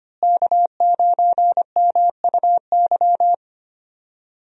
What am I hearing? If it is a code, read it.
Morse code, heard as a keyed tone, about 25 wpm, 700 Hz. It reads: K9MUY
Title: USA Call Signs - 5 Characters - Repeated 3x - 25wpm